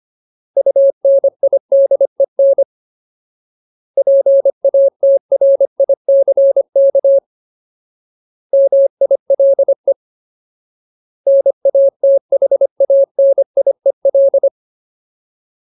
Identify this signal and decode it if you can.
Morse code, heard as a keyed tone, about 25 wpm, 550 Hz. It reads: UNIDEN PATRICK MILE NATHANIEL
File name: Ultimate Word List / 25WPM Ultimate